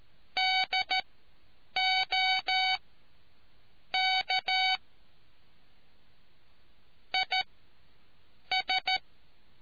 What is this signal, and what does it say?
Morse code, heard as a keyed tone, about 13 wpm, spaced as for 7 wpm, 750 Hz. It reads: DOK IS